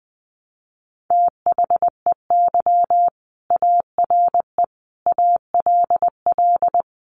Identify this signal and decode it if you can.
Morse code, heard as a keyed tone, about 20 wpm, 700 Hz. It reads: THEY ARE ALL
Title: Sentences from Top 600 Words 20wpm